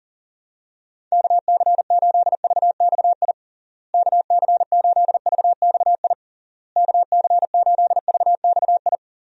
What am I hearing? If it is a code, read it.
Morse code, heard as a keyed tone, about 40 wpm, 700 Hz. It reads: KC8VXI KC8VXI KC8VXI
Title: USA Call Signs - 6 Characters - Repeated 3x - 40wpm